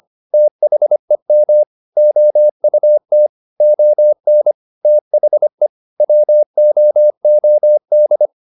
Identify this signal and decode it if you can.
Morse code, heard as a keyed tone, about 25 wpm, 600 Hz. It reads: THEM OUT ON THE WOOD
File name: the old man and the sea part 1 25wpm